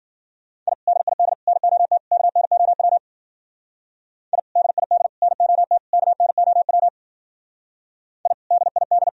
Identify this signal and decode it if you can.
Morse code, heard as a keyed tone, about 56 wpm, 700 Hz. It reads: I DID NOT KNOW I DID NOT KNOW I DID